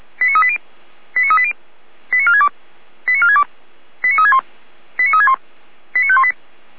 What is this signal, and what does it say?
Five-tone selective call (ZVEI), heard as quick runs of five tones, tones of about 70 ms. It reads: 89380 89380 89462 89462 89361 89361 89518